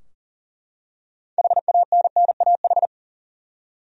Morse code, heard as a keyed tone, about 40 wpm, 700 Hz. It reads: HANNAH
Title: USA Common Names - 40wpm